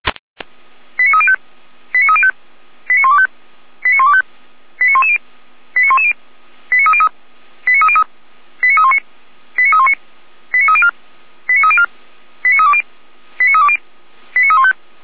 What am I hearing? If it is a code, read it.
Five-tone selective call (ZVEI), heard as quick runs of five tones, tones of about 70 ms. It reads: 89305 89305 89125 89125 89110 89110 89303 89303 89319 89319 89304 89304 89320 89320 89315